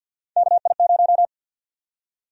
Morse code, heard as a keyed tone, about 47 wpm, 700 Hz. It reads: KI0
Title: USA Call Sign Prefixes - 50wpm